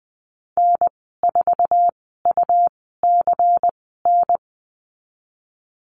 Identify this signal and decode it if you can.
Morse code, heard as a keyed tone, about 20 wpm, spaced as for 15 wpm, 700 Hz. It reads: N4UCN
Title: 128 Lesson 21g - Number 6 - Call Signs 20wpm